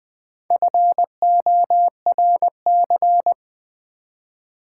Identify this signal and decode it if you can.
Morse code, heard as a keyed tone, about 20 wpm, 700 Hz. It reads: FORC
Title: Build A Word - 1 Letter At A Time - 5-Letter Words - 20wpm